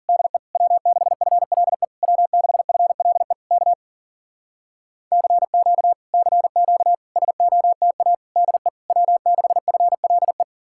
Morse code, heard as a keyed tone, about 52 wpm, 700 Hz. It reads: DE W6FLE W6FLE K CQ CQ SOTA DE W6FLE